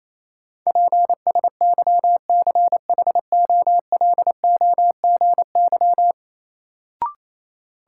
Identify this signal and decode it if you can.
Morse code, heard as a keyed tone, about 28 wpm, 700 Hz. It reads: PSYCHOLOGY
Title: Sentences from Top 1400 Words - New Word Drill Alt - 28wpm